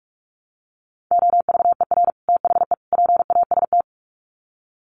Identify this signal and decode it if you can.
Morse code, heard as a keyed tone, about 44 wpm, 700 Hz. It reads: OVER THE PAST
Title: Sets of 3 Words 45wpm